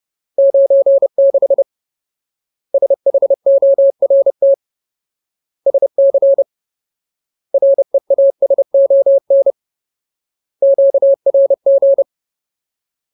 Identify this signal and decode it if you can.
Morse code, heard as a keyed tone, about 30 wpm, 550 Hz. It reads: 96 SHORT SC REASON QRG
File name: Ultimate Word List / 30WPM Ultimate